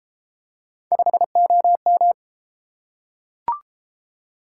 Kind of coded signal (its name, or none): Morse code